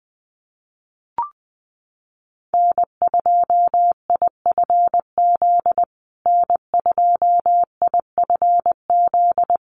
Morse code, heard as a keyed tone, about 20 wpm, 700 Hz. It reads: N2IFZ N2IFZ
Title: USA Call Signs Repeated 3x 20wpm